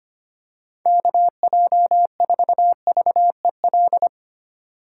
Morse code, heard as a keyed tone, about 28 wpm, 700 Hz. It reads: KJ4VEL
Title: Speed Racing - USA 2x3 Call Signs 25wpm